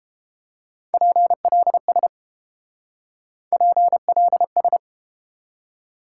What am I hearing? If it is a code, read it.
Morse code, heard as a keyed tone, about 31 wpm, 700 Hz. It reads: PLS PLS